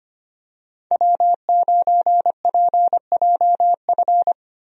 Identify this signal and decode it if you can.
Morse code, heard as a keyed tone, about 25 wpm, 700 Hz. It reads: W9PJF
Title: USA Call Signs - 5 Characters - 25wpm